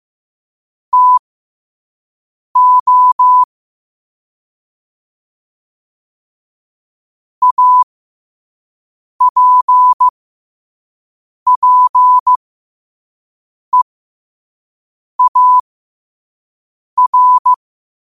Morse code, heard as a keyed tone, about 15 wpm, spaced as for 5 wpm, 1000 Hz. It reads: TO APPEAR